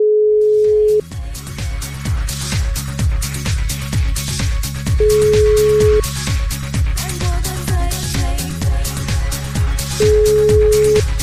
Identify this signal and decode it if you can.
Ringback tone (Europe, 425 Hz).